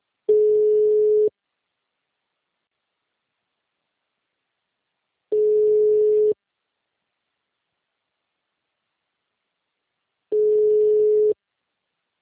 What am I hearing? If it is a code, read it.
Ringback tone (Europe, 425 Hz).
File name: out- -2004- - - .18